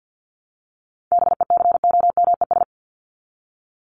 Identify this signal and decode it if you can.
Morse code, heard as a keyed tone, about 50 wpm, 700 Hz. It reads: BECOMES
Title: ES:S Suffix - Top 100 Words with Base Words 50wpm